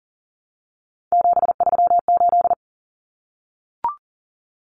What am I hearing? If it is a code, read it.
Morse code, heard as a keyed tone, about 40 wpm, 700 Hz. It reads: 738
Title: Numbers - 3-Digit 40wpm